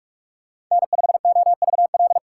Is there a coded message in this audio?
Yes: Morse code from 0.7 to 2.2 s, heard as a keyed tone, about 44 wpm, 700 Hz. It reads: N5OVL